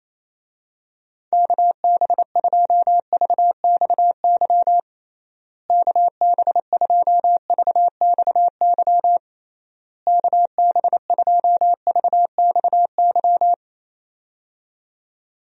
Morse code, heard as a keyed tone, about 27 wpm, 700 Hz. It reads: KB2VXY KB2VXY KB2VXY
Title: USA Call Signs - 6 Characters - Repeated 3x - 28wpm